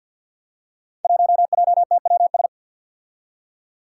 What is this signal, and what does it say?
Morse code, heard as a keyed tone, about 53 wpm, 700 Hz. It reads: 1JTWS
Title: Five Challenging Letter-Number Combinations 50wpm